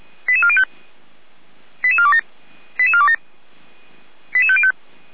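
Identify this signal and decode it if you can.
Five-tone selective call (ZVEI), heard as quick runs of five tones, tones of about 70 ms. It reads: 88405 88427 88427 88505
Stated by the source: ZVEI five-tone call 88447 received by SDR, fire brigade (Feuerwehr) Orlamünde.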